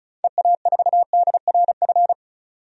Morse code, heard as a keyed tone, about 34 wpm, 700 Hz. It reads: EA4DRF
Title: DXCC-100 Call Signs - 6 Characters - 35wpm